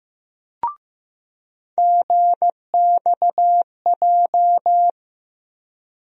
Morse code, heard as a keyed tone, about 15 wpm, 700 Hz. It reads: GXJ
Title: USA Call Sign Postfixes - 3 Letters 15wpm